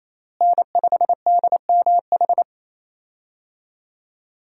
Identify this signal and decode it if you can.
Morse code, heard as a keyed tone, about 28 wpm, 700 Hz. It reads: N5DMH
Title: USA Call Signs - 5 Characters - Repeated 2x - 28wpm